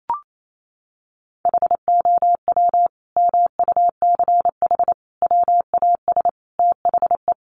Morse code, heard as a keyed tone, about 28 wpm, 700 Hz. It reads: HOW MUCH WAS THE
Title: Sentences from Top 1400 Words 28wpm